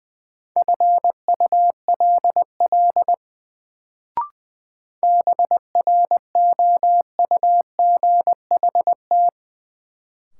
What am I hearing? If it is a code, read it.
Morse code, heard as a keyed tone, about 20 wpm, 700 Hz. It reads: FULL BROUGHT
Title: Sentences from Top 500 Words - New Word Drill - 20wpm